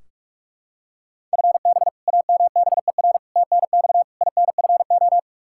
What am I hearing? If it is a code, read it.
Morse code, heard as a keyed tone, about 45 wpm, 700 Hz. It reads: FB AMBER TNX INFO
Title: Practice QSO - Exchange 3 - More Info - 45wpm